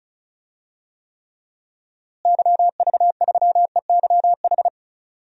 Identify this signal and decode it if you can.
Morse code, heard as a keyed tone, about 34 wpm, 700 Hz. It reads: YV3EYH
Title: DXCC-100 Call Signs - 6 Characters - Repeated 2x - 35wpm